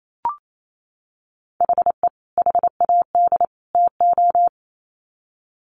Morse code, heard as a keyed tone, about 28 wpm, 700 Hz. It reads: HE HAD TO
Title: Sets of 3 Words 28wpm